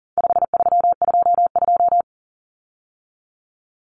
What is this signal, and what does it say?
Morse code, heard as a keyed tone, about 37 wpm, 700 Hz. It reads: 5322
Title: Common Cut Numbers - 4-Digit 40wpm